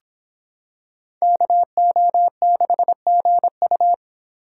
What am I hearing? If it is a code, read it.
Morse code, heard as a keyed tone, about 26 wpm, 700 Hz. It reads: KO6GU